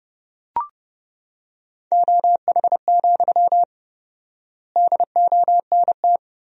Morse code, heard as a keyed tone, about 32 wpm, 700 Hz. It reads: OH, DONT
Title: 4x Word-Spacing - Sentences from Top 500 Words - 30wpm